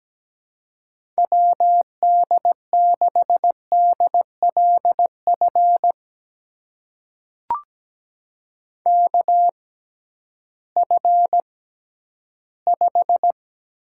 Morse code, heard as a keyed tone, about 17 wpm, spaced as for 11 wpm, 700 Hz. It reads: WD6DLF KF5
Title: USA Call Signs - 6 Characters - 6x Character Spacing - 17wpm